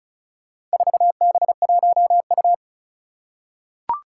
Morse code, heard as a keyed tone, about 34 wpm, 700 Hz. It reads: 4B1U
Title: Four Challenging Letter-Number Combinations 35wpm